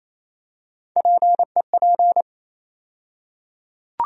Morse code, heard as a keyed tone, about 28 wpm, 700 Hz. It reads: PEP
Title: USA Call Sign Postfixes - 3 Letters 28wpm